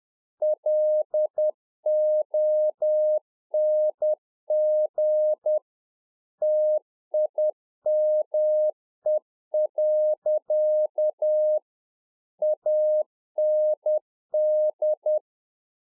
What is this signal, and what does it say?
Morse code, heard as a keyed tone, about 10 wpm, 610 Hz. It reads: LONG TIME. AND
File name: 2011-04-mp3 / A SMALL TRANCEIVER 1 10WPM